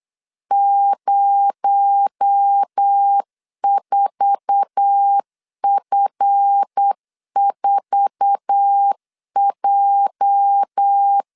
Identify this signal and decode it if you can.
Morse code, heard as a keyed tone, about 8 wpm, 800 Hz. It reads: 04F4J
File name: LEZIONI / 8 YJ45 / 8 YJ4 8wpm 3246